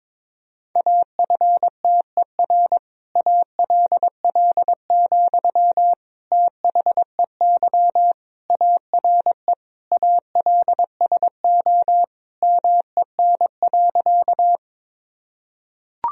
Morse code, heard as a keyed tone, about 22 wpm, 700 Hz. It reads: AFTER ALL, THEY ARE ALSO MEN.
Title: Sentences from Top 1000 Words - Review Part 1 - 22wpm